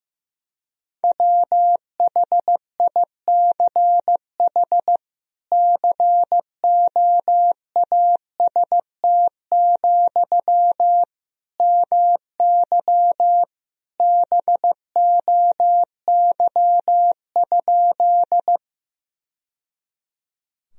Morse code, heard as a keyed tone, about 15 wpm, 700 Hz. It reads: WHICH COAST, MY BOY?
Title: Sentences from Top 1000 Words - Review Part 4 - 15wpm